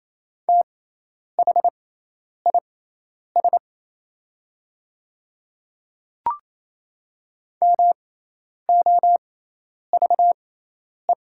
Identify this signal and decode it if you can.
Morse code, heard as a keyed tone, about 28 wpm, spaced as for 10 wpm, 700 Hz. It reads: THIS MOVE
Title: 6x Character Spacing - Top 300 Words - 28wpm